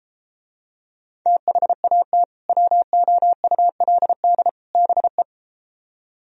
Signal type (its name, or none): Morse code